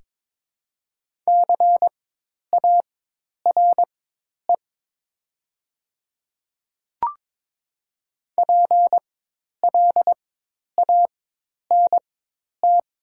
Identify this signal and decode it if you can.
Morse code, heard as a keyed tone, about 21 wpm, spaced as for 10 wpm, 700 Hz. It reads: CARE PLANT
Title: 4x Character Spacing - Top 300 Words - 22wpm